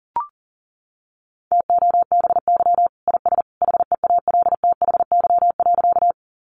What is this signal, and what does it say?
Morse code, heard as a keyed tone, about 40 wpm, 700 Hz. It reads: TOBY IS HEALTHY.
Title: Easy Phrases 40wpm